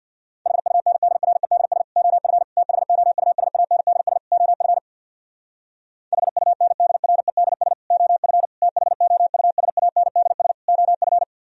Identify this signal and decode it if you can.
Morse code, heard as a keyed tone, about 60 wpm, 700 Hz. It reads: HUNDREDS OF THOUSANDS OF HUNDREDS OF THOUSANDS OF
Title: Speed-Racing - Sets of 4 Words 50wpm